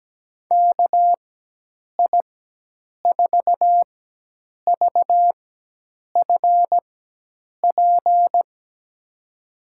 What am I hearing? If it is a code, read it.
Morse code, heard as a keyed tone, about 17 wpm, spaced as for 8 wpm, 700 Hz. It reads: KI4VFP
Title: USA Call Signs - 6 Characters - 4x Character Spacing - 17wpm